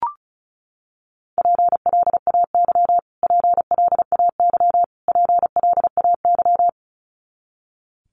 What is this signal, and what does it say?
Morse code, heard as a keyed tone, about 34 wpm, 700 Hz. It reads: PLAY PLAY PLAY
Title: Top 200 Words Repeated 35wpm